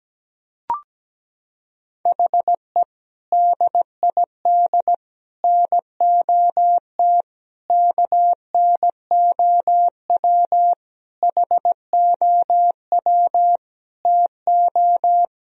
Morse code, heard as a keyed tone, about 17 wpm, 700 Hz. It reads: HE DID NOT KNOW HOW TO
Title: Sentences from Top 1000 Words - Review Part 1 - 17wpm